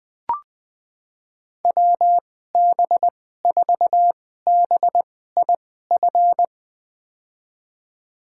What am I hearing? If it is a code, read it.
Morse code, heard as a keyed tone, about 20 wpm, spaced as for 14 wpm, 700 Hz. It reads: WB4BIF
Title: USA Call Signs - 6 Characters - 2x Character Spacing - 20wpm